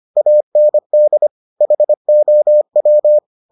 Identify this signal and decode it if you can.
Morse code, heard as a keyed tone, about 25 wpm, 600 Hz. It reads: AND HOW